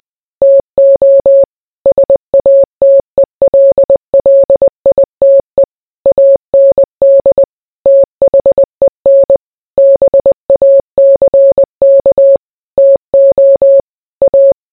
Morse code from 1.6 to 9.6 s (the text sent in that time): SATELLITE AND THEN